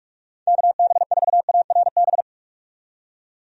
Morse code, heard as a keyed tone, about 43 wpm, 700 Hz. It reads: KB4ARB
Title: USA Call Signs - 6 Characters - 45wpm